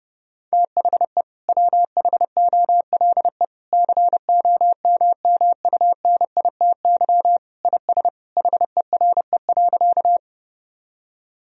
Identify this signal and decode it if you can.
Morse code, heard as a keyed tone, about 30 wpm, 700 Hz. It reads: THE WHOLE COMMUNITY IS HERE.
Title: Sentences from Top 1000 Words - Review Part 4 - 30wpm